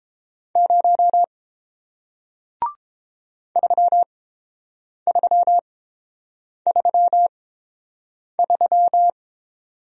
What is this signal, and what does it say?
Morse code, heard as a keyed tone, about 30 wpm, spaced as for 7 wpm, 690 Hz. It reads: 0 3333